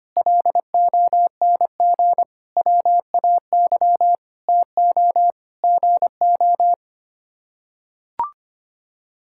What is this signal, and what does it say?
Morse code, heard as a keyed tone, about 25 wpm, 700 Hz. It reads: LONG WAY TO GO
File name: Sets of 9 Words - 25wpm